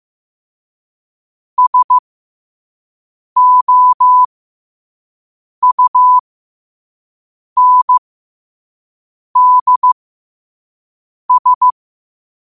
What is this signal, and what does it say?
Morse code, heard as a keyed tone, about 15 wpm, spaced as for 5 wpm, 1000 Hz. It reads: SOUNDS
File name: 5wpm